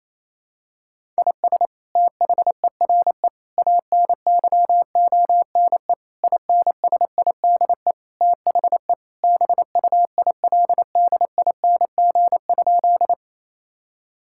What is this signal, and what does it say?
Morse code, heard as a keyed tone, about 28 wpm, 700 Hz. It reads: IS THERE ANYONE INSIDE THE BUILDING?
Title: Sentences from Top 1000 Words - Review Part 6 - 28wpm